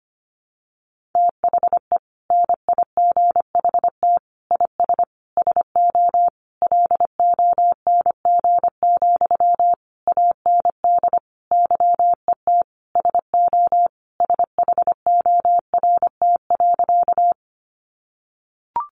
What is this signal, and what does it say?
Morse code, heard as a keyed tone, about 25 wpm, 700 Hz. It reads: THE NIGHT IS SO LONG, AND YET SO SHORT.
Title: Sentences from Top 500 Words 25wpm